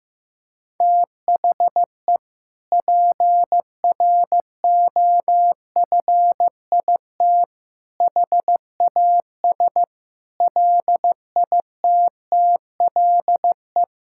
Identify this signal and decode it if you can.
Morse code, heard as a keyed tone, about 15 wpm, 700 Hz. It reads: THE PROFIT HAS LITTLE
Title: Sentences from Top 1000 Words - Review Part 5 - 15wpm